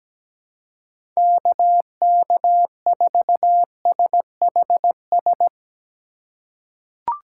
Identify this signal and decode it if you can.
Morse code, heard as a keyed tone, about 17 wpm, 700 Hz. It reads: KK4SHS
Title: USA Call Signs - 6 Characters - 4x Character Spacing - 17wpm